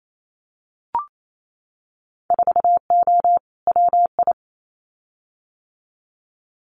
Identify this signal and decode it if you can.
Morse code, heard as a keyed tone, about 28 wpm, 700 Hz. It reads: 4O WI
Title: Winter Field Day Exchanges 28wpm